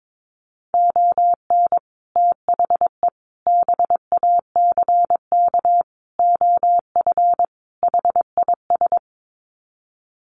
Morse code, heard as a keyed tone, about 22 wpm, 700 Hz. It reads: ON THE BACK OF HIS
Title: Sets of 5 Words Encore 22wpm